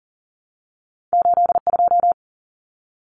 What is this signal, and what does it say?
Morse code, heard as a keyed tone, about 40 wpm, 700 Hz. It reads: 82
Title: Numbers - 2-Digit 40wpm